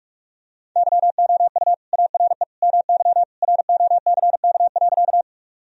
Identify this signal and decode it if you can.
Morse code, heard as a keyed tone, about 46 wpm, 700 Hz. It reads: YOU ARE MY ROCK.
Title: Sentences from Top 1000 Words - Review Part 2 - 45wpm